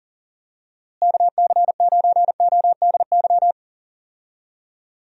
Morse code, heard as a keyed tone, about 40 wpm, 700 Hz. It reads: KC9ODY
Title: USA Call Signs - 6 Characters - Repeated 3x - 40wpm